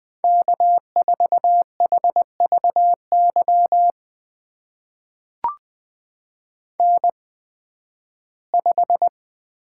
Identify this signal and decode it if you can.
Morse code, heard as a keyed tone, about 20 wpm, spaced as for 13 wpm, 700 Hz. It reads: K4HVY N5
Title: USA Call Signs - 5 Characters - 8x Character Spacing - 20wpm